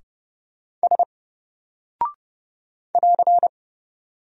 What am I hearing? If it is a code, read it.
Morse code, heard as a keyed tone, about 30 wpm, 700 Hz. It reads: S <AR>
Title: Single Character (Letter:Number:Punctuation:Prosign) 30wpm